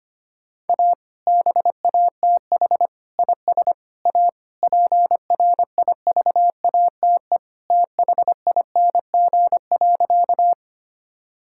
Morse code, heard as a keyed tone, about 25 wpm, 700 Hz. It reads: A BATH IS A PRIVATE THING.
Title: Sentences from Top 1600 Words 25wpm